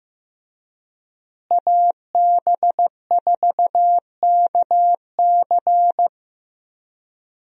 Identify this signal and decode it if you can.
Morse code, heard as a keyed tone, about 15 wpm, 700 Hz. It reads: AB4KC
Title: USA Call Signs - 5 Characters - Repeated 2x - 15wpm